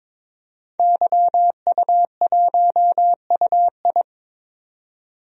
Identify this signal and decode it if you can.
Morse code, heard as a keyed tone, about 22 wpm, 700 Hz. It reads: YU1UI
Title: DXCC-100 Call Signs - 5 Characters - 22wpm